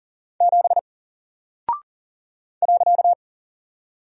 Morse code, heard as a keyed tone, about 40 wpm, 700 Hz. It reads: 7 .
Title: Single Character (Letter:Number:Punctuation:Prosign) 40wpm